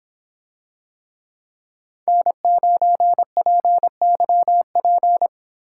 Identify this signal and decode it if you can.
Morse code, heard as a keyed tone, about 25 wpm, 700 Hz. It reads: N9PYP